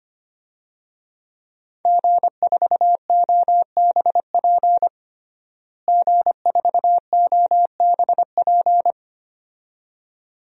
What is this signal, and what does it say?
Morse code, heard as a keyed tone, about 25 wpm, 700 Hz. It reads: G4OBP G4OBP
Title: DXCC-100 Call Signs - 5 Characters - Repeated 2x - 25wpm